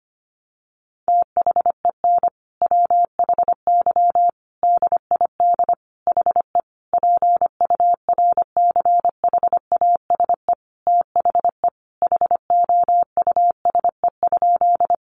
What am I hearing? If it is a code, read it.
Morse code, heard as a keyed tone, about 25 wpm, 700 Hz. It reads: THEN WHY DID HE PURCHASE THE HOUSE?
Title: Sentences from Top 800 Words 25wpm